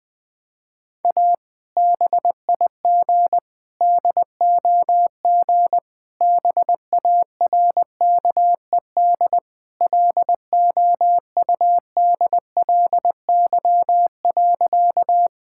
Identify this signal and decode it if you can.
Morse code, heard as a keyed tone, about 20 wpm, 700 Hz. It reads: A BIG DOG BARKED LOUDLY.
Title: Sentences with 2-Word Phrases - 20wpm